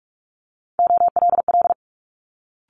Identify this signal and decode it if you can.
Morse code, heard as a keyed tone, about 46 wpm, 700 Hz. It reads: OLL
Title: Multi-letter Phonemes Quick Fire 45wpm